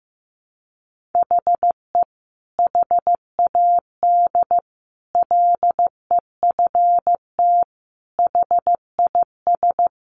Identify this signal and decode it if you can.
Morse code, heard as a keyed tone, about 15 wpm, 700 Hz. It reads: HE HAD LEFT HIS
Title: Sentences from Top 1500 Words 15wpm